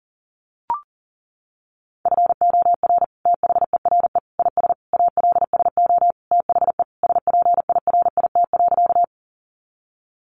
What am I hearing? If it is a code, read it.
Morse code, heard as a keyed tone, about 40 wpm, 700 Hz. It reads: FOR THERE IS ALSO THE SPIRIT.
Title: Sentences from Top 1400 Words 40wpm